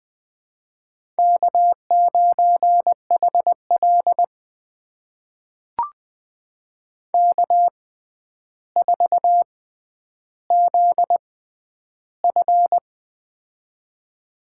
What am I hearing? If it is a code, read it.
Morse code, heard as a keyed tone, about 20 wpm, spaced as for 10 wpm, 700 Hz. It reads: K9HL K4ZF
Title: USA Call Signs - 4 Characters - 6x Character Spacing - 20wpm